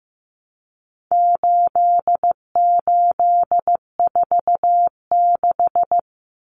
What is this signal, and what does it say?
Morse code, heard as a keyed tone, about 14 wpm, 700 Hz. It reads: 8846
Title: Numbers - 4-Digit 15wpm